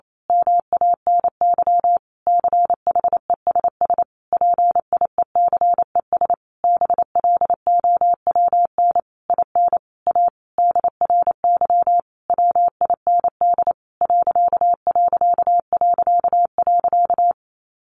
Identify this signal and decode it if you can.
Morse code, heard as a keyed tone, about 28 wpm, 700 Hz. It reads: MANY CHESS PIECES BLOWN IN A DRY WIND ....
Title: A Sound Of Thunder 28wpm